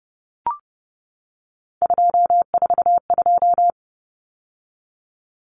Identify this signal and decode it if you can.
Morse code, heard as a keyed tone, about 30 wpm, 700 Hz. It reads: 242
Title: Numbers - 3-Digit 30wpm